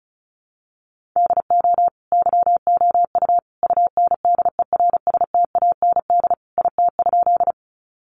Morse code, heard as a keyed tone, about 34 wpm, 700 Hz. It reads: DO YOU UNDERSTAND IT?
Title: Sentences from Top 600 Words 35wpm